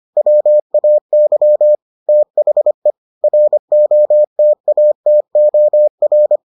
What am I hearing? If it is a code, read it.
Morse code, heard as a keyed tone, about 25 wpm, 600 Hz. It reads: WAY THE ROTATOR